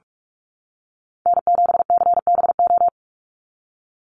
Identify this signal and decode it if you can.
Morse code, heard as a keyed tone, about 44 wpm, 700 Hz. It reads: N7CBO
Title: USA Call Signs 45wpm